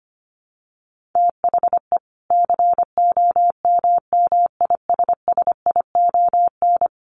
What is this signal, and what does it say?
Morse code, heard as a keyed tone, about 25 wpm, 700 Hz. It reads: THE COMMISSION
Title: Sentences from Top 1400 Words 25wpm